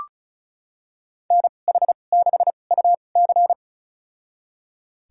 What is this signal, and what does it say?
Morse code, heard as a keyed tone, about 36 wpm, spaced as for 25 wpm, 700 Hz. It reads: NH6UC
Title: USA Call Signs - 2x Character Spacing - 35wpm